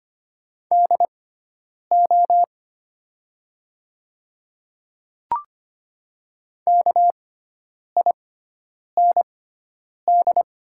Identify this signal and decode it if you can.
Morse code, heard as a keyed tone, about 25 wpm, spaced as for 9 wpm, 700 Hz. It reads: DO KIND
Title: 6x Character Spacing - Top 200 Words - 25wpm